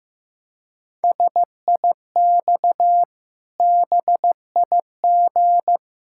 Morse code, heard as a keyed tone, about 15 wpm, 700 Hz. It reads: SIX BIG
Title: Pangram Challenge v2 - 15wpm